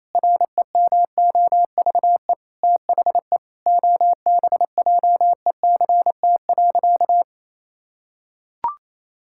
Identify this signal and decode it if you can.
Morse code, heard as a keyed tone, about 28 wpm, 700 Hz. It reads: REMOVE THE OBJECT.
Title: Sentences from Top 1000 Words - Review Part 5 - 28wpm